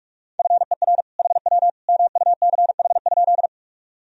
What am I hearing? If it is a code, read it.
Morse code, heard as a keyed tone, about 41 wpm, 700 Hz. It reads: FER HW MUCH?
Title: Sentences from Top 200 Words and Abbreviations 45wpm